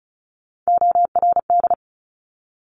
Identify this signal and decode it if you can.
Morse code, heard as a keyed tone, about 36 wpm, 700 Hz. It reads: ORD
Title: Top Three Letter Combinations in Words 35wpm